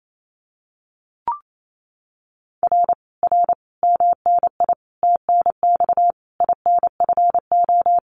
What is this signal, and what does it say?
Morse code, heard as a keyed tone, about 28 wpm, 700 Hz. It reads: R R MNI TNX INFO
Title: Common QSO Phrases Encore 28wpm